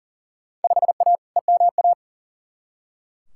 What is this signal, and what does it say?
Morse code, heard as a keyed tone, about 37 wpm, 700 Hz. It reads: 5A EMA
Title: Field Day Exchanges Encore - 40wpm